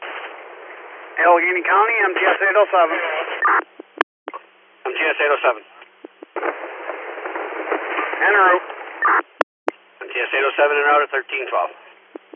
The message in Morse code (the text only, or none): none